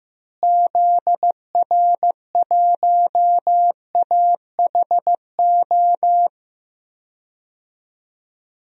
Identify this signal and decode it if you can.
Morse code, heard as a keyed tone, about 16 wpm, 700 Hz. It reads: ZR1AHO
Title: DXCC-100 Call Signs - 6 Characters - Repeated 2x - 15wpm